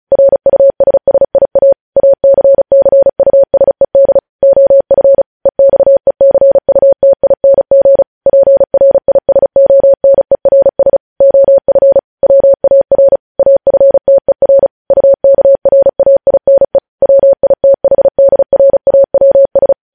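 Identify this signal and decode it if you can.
Morse code, heard as a keyed tone, about 36 wpm, 550 Hz. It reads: RUSSIA ACCUSED OF EXECUTING PRISONERS OF WAR AFTER UKRAINE WITHDRAWS
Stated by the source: Morse news headlines practice at 35.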